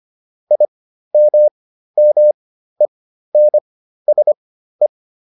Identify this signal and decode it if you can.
Morse code, heard as a keyed tone, about 25 wpm, spaced as for 13 wpm, 600 Hz. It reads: IMMENSE